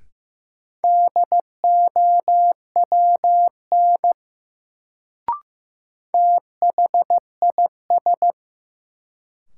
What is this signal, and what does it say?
Morse code, heard as a keyed tone, about 15 wpm, 700 Hz. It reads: DOWN THIS
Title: Sets of 2 Words - New Word Drill - 15wpm